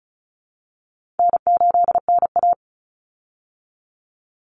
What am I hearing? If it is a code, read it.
Morse code, heard as a keyed tone, about 34 wpm, 700 Hz. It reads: N8NA
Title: Easy Call Signs Repeated 3x - 35wpm